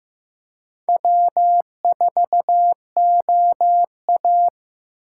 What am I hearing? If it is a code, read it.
Morse code, heard as a keyed tone, about 15 wpm, 700 Hz. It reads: W4OA
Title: USA Call Signs - 4 Characters - 4x Character Spacing - 15wpm